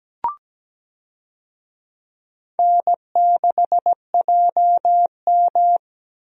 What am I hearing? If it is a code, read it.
Morse code, heard as a keyed tone, about 17 wpm, 700 Hz. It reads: N6JM
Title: USA Call Signs - 4 Characters - 17wpm